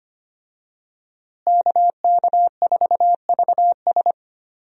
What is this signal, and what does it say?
Morse code, heard as a keyed tone, about 25 wpm, 700 Hz. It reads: KK4VS
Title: USA Call Signs - 5 Characters - Repeated 3x - 25wpm